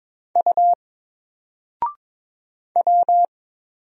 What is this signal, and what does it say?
Morse code, heard as a keyed tone, about 22 wpm, 700 Hz. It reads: U W
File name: Single-Letter-Number-22wpm